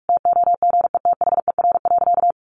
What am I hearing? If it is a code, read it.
Morse code, heard as a keyed tone, about 44 wpm, 700 Hz. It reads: TOGETHER.
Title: Sentences from Top 1000 Words 45wpm